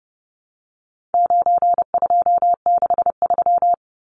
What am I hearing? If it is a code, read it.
Morse code, heard as a keyed tone, about 30 wpm, 700 Hz. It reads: 9263
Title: Numbers - 4-Digit 30wpm